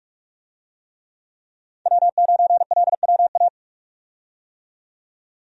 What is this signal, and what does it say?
Morse code, heard as a keyed tone, about 42 wpm, 700 Hz. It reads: W9LWA
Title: USA Call Signs - 5 Characters - 45wpm